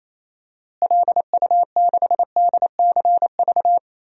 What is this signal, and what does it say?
Morse code, heard as a keyed tone, about 28 wpm, 700 Hz. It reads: LU6DCV
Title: DXCC-100 Call Signs - 6 Characters - Repeated 2x - 28wpm